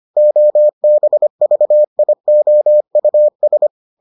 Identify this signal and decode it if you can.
Morse code, heard as a keyed tone, about 25 wpm, 600 Hz. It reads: OBVIOUS